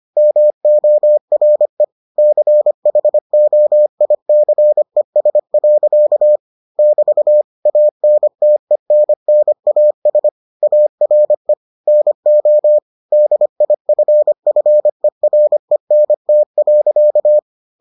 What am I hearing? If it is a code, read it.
Morse code, heard as a keyed tone, about 25 wpm, 600 Hz. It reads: MORE CHOICES. <BT> ANTENNAS ARE NO DIFFERENT.